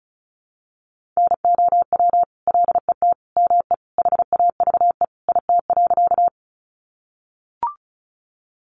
Morse code, heard as a keyed tone, about 34 wpm, 700 Hz. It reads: NOW LET ME HAVE IT.
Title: Sentences from Top 300 Words 35wpm